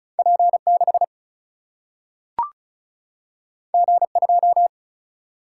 Morse code, heard as a keyed tone, about 36 wpm, 700 Hz. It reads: P6 G2
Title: Single-Letter Single-Number 35wpm